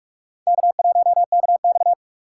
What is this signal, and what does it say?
Morse code, heard as a keyed tone, about 44 wpm, 690 Hz. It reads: K1KX
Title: USA Call Signs - 4 Characters - 2x Character Spacing - 45wpm